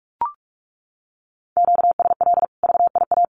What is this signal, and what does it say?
Morse code, heard as a keyed tone, about 44 wpm, 700 Hz. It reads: QSL VIA
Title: Common QSO Phrases 45wpm